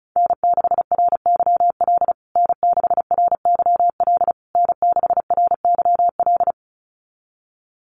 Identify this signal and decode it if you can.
Morse code, heard as a keyed tone, about 34 wpm, 700 Hz. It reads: N6RYL N6RYL N6RYL
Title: USA Call Signs Repeated 3x 35wpm